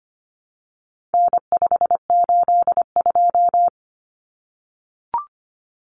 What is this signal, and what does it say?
Morse code, heard as a keyed tone, about 25 wpm, 700 Hz. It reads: N582
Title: Common Cut Numbers - 4-Digit 25wpm